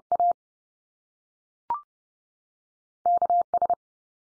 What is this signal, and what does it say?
Morse code, heard as a keyed tone, about 30 wpm, 700 Hz. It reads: A KS
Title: US States 30WPM